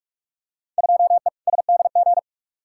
Morse code, heard as a keyed tone, about 46 wpm, 700 Hz. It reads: 2E SDG
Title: Field Day Exchanges - 45wpm